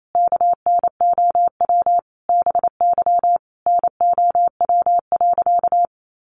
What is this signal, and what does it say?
Morse code, heard as a keyed tone, about 28 wpm, 700 Hz. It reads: KNOW BY NOW.
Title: Sentences from Top 1400 Words 28wpm